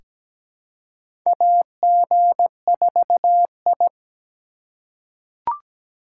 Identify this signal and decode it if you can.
Morse code, heard as a keyed tone, about 17 wpm, 700 Hz. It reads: AG4I
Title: USA Call Signs - 2x Character Spacing - 17wpm